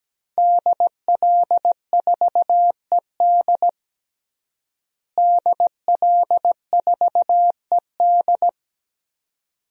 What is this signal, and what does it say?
Morse code, heard as a keyed tone, about 17 wpm, 700 Hz. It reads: DL4ED DL4ED
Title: DXCC-100 Call Signs - 5 Characters - Repeated 2x - 17wpm